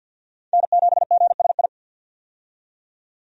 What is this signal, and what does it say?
Morse code, heard as a keyed tone, about 46 wpm, 700 Hz. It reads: N7GSI
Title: USA Call Signs - 5 Characters - 50wpm